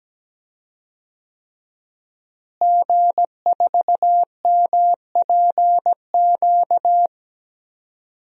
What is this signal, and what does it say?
Morse code, heard as a keyed tone, about 17 wpm, 700 Hz. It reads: G4MPQ
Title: DXCC-100 Call Signs - 5 Characters - Repeated 3x - 17wpm